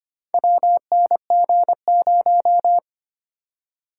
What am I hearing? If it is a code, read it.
Morse code, heard as a keyed tone, about 25 wpm, 700 Hz. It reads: WNG0
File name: practice / 25wpm / Four Letter-Number Combinations - 25wpm